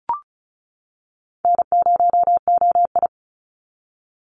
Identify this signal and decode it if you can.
Morse code, heard as a keyed tone, about 34 wpm, 700 Hz. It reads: N0OI
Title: NA SOTA Activator Callsigns 35wpm